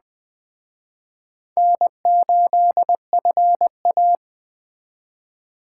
Morse code, heard as a keyed tone, about 20 wpm, 700 Hz. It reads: N8FA
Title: Speed Racing - USA 1x2 Call Signs 17wpm